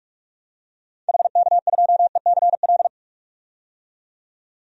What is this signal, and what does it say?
Morse code, heard as a keyed tone, about 46 wpm, 700 Hz. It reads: HK2ECL
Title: DXCC-100 Call Signs - 6 Characters - 45wpm